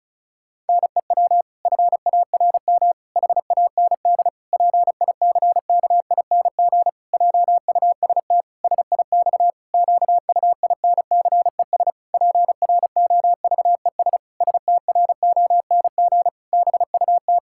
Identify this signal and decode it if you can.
Morse code, heard as a keyed tone, about 36 wpm, 700 Hz. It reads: NEW FARM HAND PICKING JUST SIX QUINCES PROVES STRONG BUT
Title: Pangram Challenge v2 - 35wpm